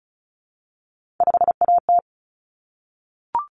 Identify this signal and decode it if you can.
Morse code, heard as a keyed tone, about 34 wpm, 700 Hz. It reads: 5AT
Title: Common Cut Numbers - 3-Digit 35wpm